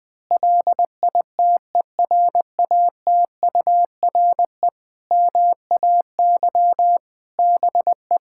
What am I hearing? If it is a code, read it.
Morse code, heard as a keyed tone, about 20 wpm, 700 Hz. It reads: LITERATURE MAY BE
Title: Sentences from Top 1600 Words 20wpm